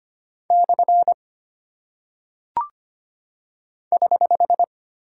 Morse code, heard as a keyed tone, about 25 wpm, 700 Hz. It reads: / <HH>
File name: Single Punctuation-Prosign - 25wpm